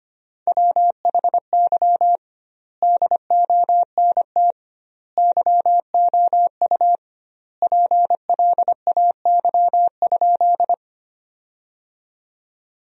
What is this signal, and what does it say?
Morse code, heard as a keyed tone, about 25 wpm, 700 Hz. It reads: WHY DONT YOU PLAY?
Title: 2x Word-Spacing - Sentences from Top 300 Words - 25wpm